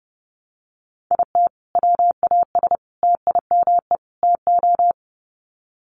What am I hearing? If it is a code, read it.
Morse code, heard as a keyed tone, about 30 wpm, 700 Hz. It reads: IT WAS TIME TO
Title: Sets of 4 Words 30wpm